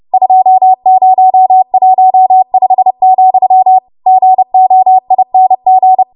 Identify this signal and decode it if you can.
Morse code, heard as a keyed tone, about 30 wpm, 750 Hz. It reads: 2015, GOING
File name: Morse / 30WPM